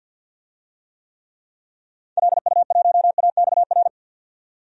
Morse code, heard as a keyed tone, about 51 wpm, 700 Hz. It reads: LU1AXR
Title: DXCC-100 Call Signs - 6 Characters - 50wpm